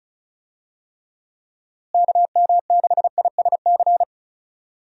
Morse code, heard as a keyed tone, about 34 wpm, 700 Hz. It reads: KM6ISC
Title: USA Call Signs - 6 Characters - Repeated 3x - 35wpm